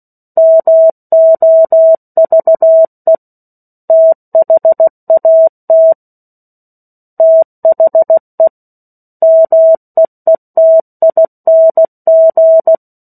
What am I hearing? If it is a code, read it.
Morse code, heard as a keyed tone, about 16 wpm, 650 Hz. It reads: MOVE THAT THE MEETING
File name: 03-Chapter-03 16 WPM